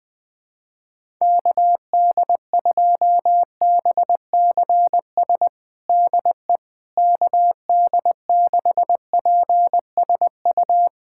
Morse code, heard as a keyed tone, about 20 wpm, 700 Hz. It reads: KD2BCS DE KD6PSU
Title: USA Callsign DE Callsign 20wpm